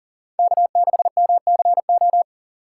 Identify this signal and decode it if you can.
Morse code, heard as a keyed tone, about 40 wpm, 700 Hz. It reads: K6MCO
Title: USA Call Signs - 5 Characters - 6x Character Spacing - 40wpm